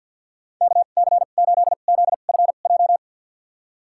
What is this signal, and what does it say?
Morse code, heard as a keyed tone, about 46 wpm, spaced as for 35 wpm, 700 Hz. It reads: KC8ZFJ
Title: USA Call Signs - 2x Character Spacing - 50wpm